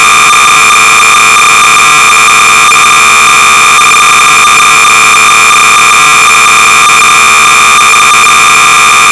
Cassette data (Kansas City type).
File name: sorcerer / 300baud / Bio-rhythm(Micro Software)(Sorcerer Users (Syndey) Group)-BASIC-Sorcerer-300baud